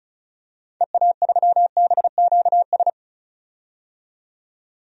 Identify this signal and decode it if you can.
Morse code, heard as a keyed tone, about 36 wpm, 700 Hz. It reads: EA3BQS
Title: DXCC-100 Call Signs - 6 Characters - 35wpm